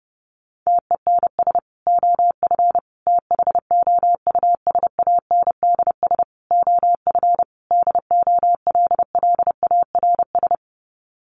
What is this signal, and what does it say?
Morse code, heard as a keyed tone, about 30 wpm, 700 Hz. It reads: TENS OF THOUSANDS OF DOLLARS
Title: Sets of 5 Words Encore2 30wpm